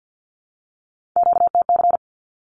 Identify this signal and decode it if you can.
Morse code, heard as a keyed tone, about 48 wpm, 700 Hz. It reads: QTC
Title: Q-Codes 50wpm